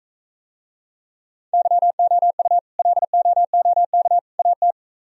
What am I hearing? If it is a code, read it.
Morse code, heard as a keyed tone, about 45 wpm, 700 Hz. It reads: YOU LOOK AT